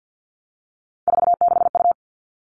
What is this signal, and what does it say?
Morse code, heard as a keyed tone, about 50 wpm, 700 Hz. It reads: 46U
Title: All Cut Numbers - 3-Digit 50wpm